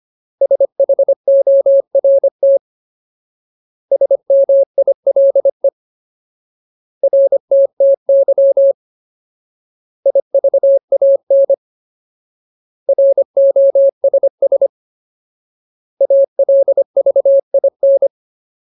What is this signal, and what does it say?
Morse code, heard as a keyed tone, about 25 wpm, 550 Hz. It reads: SHORT SMILE RTTY IVAN ROSS ALVIN
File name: Ultimate Word List / 25WPM Ultimate